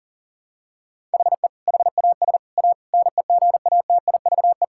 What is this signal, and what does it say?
Morse code, heard as a keyed tone, about 37 wpm, 700 Hz. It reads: HE HAS A NEGATIVE
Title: Sentences from Top 1000 Words - Review Part 6 - 40wpm